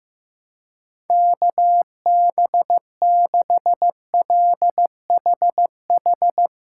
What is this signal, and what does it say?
Morse code, heard as a keyed tone, about 15 wpm, 700 Hz. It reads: KB6LHH
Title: USA Call Signs - 6 Characters - 6x Character Spacing - 15wpm